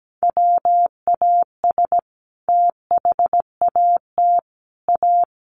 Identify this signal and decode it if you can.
Morse code, heard as a keyed tone, about 17 wpm, 700 Hz. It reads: WAS THAT A
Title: Sentences from Top 1400 Words 17wpm